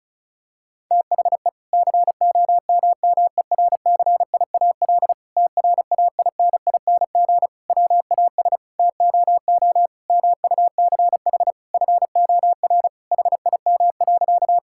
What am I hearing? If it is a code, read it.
Morse code, heard as a keyed tone, about 36 wpm, 700 Hz. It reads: THE COMMERCIAL TRAINING WAS TOO MUCH FOR HIM.
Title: Sentences from Top 1000 Words - Review Part 5 - 35wpm